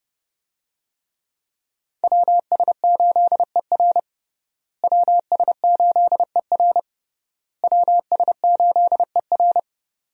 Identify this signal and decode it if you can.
Morse code, heard as a keyed tone, about 29 wpm, 700 Hz. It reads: WS8ER WS8ER WS8ER
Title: USA Call Signs - 5 Characters - Repeated 3x - 30wpm